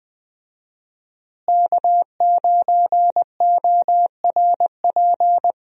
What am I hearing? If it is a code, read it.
Morse code, heard as a keyed tone, about 21 wpm, 700 Hz. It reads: K9ORP
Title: USA Call Signs - 5 Characters - Repeated 2x - 20wpm